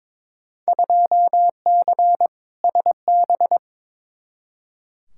Morse code, heard as a keyed tone, about 21 wpm, 700 Hz. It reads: 2C SB
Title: Field Day Exchanges Encore - 22wpm